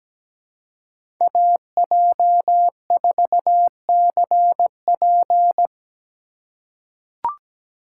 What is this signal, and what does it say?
Morse code, heard as a keyed tone, about 17 wpm, 700 Hz. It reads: AJ4CP
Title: USA Call Signs - 3x Character Spacing - 17wpm